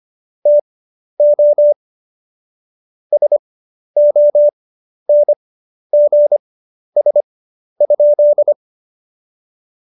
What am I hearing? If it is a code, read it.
Morse code, heard as a keyed tone, about 25 wpm, spaced as for 11 wpm, 590 Hz. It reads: TO SONGS?
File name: Jokes-25-char-10-eff